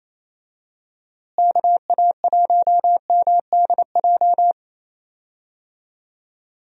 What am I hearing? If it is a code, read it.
Morse code, heard as a keyed tone, about 28 wpm, 700 Hz. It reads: KA1MDJ
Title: USA Call Signs - 6 Characters - Repeated 3x - 28wpm